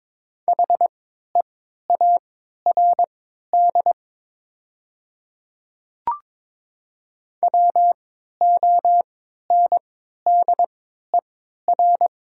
Morse code, heard as a keyed tone, about 22 wpm, spaced as for 12 wpm, 700 Hz. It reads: HEARD WONDER
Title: 3x Character Spacing - Top 500 Words - 22wpm